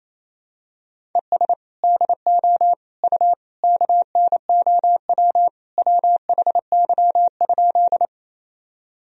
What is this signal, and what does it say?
Morse code, heard as a keyed tone, about 28 wpm, 700 Hz. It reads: ES DO U KNOW WHY?
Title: Sentences from Top 200 Words and Abbreviations 28wpm